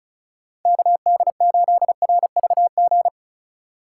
Morse code, heard as a keyed tone, about 36 wpm, 700 Hz. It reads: KD8RVG
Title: USA Call Signs - 6 Characters - 3x Character Spacing - 35wpm